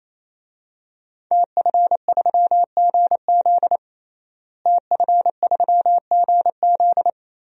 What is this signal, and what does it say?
Morse code, heard as a keyed tone, about 28 wpm, 700 Hz. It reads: TF3GZ TF3GZ
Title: DXCC-100 Call Signs - 5 Characters - Repeated 3x - 28wpm